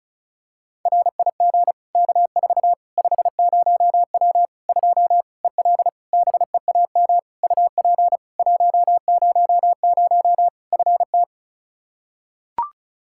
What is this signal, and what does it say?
Morse code, heard as a keyed tone, about 36 wpm, 700 Hz. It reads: RIG K4 50W 2 EL BEAM UP 100 FT
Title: Practice QSO - Exchange 2 - Info - 35wpm